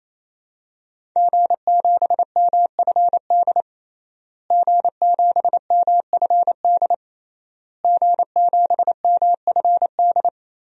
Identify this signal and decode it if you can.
Morse code, heard as a keyed tone, about 28 wpm, 700 Hz. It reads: G7MFD G7MFD G7MFD
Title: DXCC-100 Call Signs - 5 Characters - Repeated 3x - 28wpm